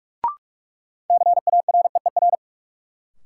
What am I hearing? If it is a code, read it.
Morse code, heard as a keyed tone, about 43 wpm, 700 Hz. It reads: CAREER
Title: Sentences from Top 1000 Words - New Word Drill - 45wpm